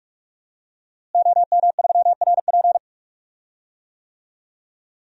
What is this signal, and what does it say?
Morse code, heard as a keyed tone, about 46 wpm, 700 Hz. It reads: OM3RP
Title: DXCC-100 Call Signs - 5 Characters - Repeated 3x - 45wpm